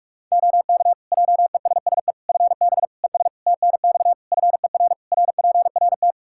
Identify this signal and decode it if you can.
Morse code, heard as a keyed tone, about 42 wpm, 700 Hz. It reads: OK JESSE FB ES TNX FER RPRT
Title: Practice QSO - Exchange 2 - Info - 45wpm